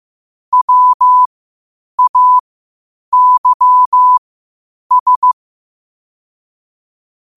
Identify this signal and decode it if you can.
Morse code, heard as a keyed tone, about 15 wpm, spaced as for 8 wpm, 1000 Hz. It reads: WAYS
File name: qotdfiles / 8wpm